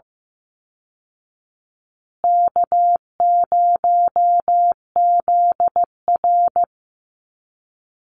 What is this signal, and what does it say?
Morse code, heard as a keyed tone, about 15 wpm, 700 Hz. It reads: K0ZR
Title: Easy Call Signs Repeated 3x - 15wpm